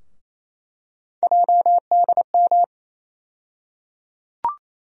Morse code, heard as a keyed tone, about 28 wpm, 700 Hz. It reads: JDM
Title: USA Call Sign Postfixes - 3 Letters 28wpm